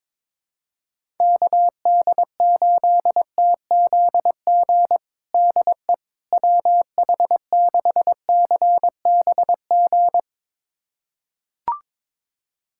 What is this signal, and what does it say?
Morse code, heard as a keyed tone, about 23 wpm, 700 Hz. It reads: KD8TZG DE WH6CBG
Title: USA Callsign DE Callsign - 6 Characters - 22wpm